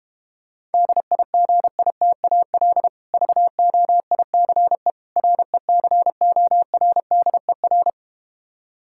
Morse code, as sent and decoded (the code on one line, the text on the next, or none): -.. .. --. .. - .- .-.. / ...- --- .. -.-. / .-. -.-. --- .-. -.. .-.
DIGITAL VOICE RECORDER